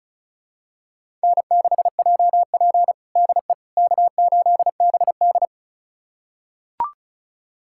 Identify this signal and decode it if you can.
Morse code, heard as a keyed tone, about 36 wpm, 700 Hz. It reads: N6JP DE K8BD
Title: USA Callsign DE Callsign - 4 Characters - 35wpm